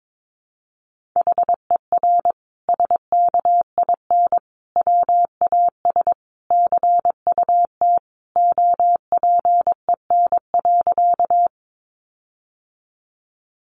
Morse code, heard as a keyed tone, about 22 wpm, 700 Hz. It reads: HER SKIN WAS CUT OPEN.
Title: Sentences from Top 1300 Words 22wpm